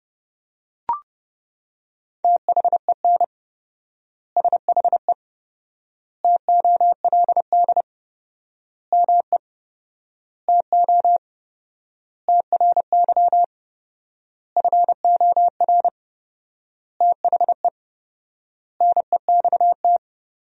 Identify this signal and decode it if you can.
Morse code, heard as a keyed tone, about 30 wpm, 700 Hz. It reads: THEN SHE TOLD ME TO TRY FOR THE NEXT
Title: 4x Word-Spacing - Sentences from Top 400 Words - 30wpm